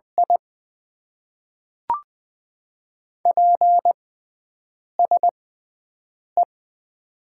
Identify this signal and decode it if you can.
Morse code, heard as a keyed tone, about 20 wpm, spaced as for 7 wpm, 700 Hz. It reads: I PSE
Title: Top 90 QSO Elements - 6x Character Spacing - 20wpm